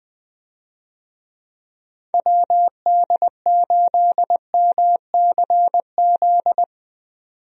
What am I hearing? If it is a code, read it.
Morse code, heard as a keyed tone, about 20 wpm, 700 Hz. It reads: WD8MCZ
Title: USA Call Signs - 6 Characters - 20wpm